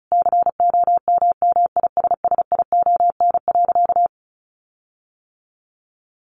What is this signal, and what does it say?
Morse code, heard as a keyed tone, about 34 wpm, 700 Hz. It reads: COMMISSION.
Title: Sentences from Top 1400 Words 35wpm